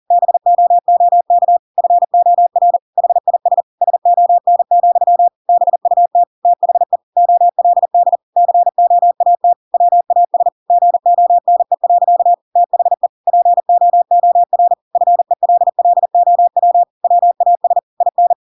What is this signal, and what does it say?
Morse code, heard as a keyed tone, about 40 wpm, 700 Hz. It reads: BOOK FOR HIS SON, BUT THE OLD COAT WAS GONE. THE POOR FELLOW WAS IN
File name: Pinocchio / Pinocchio ch8 40 40 ik1hge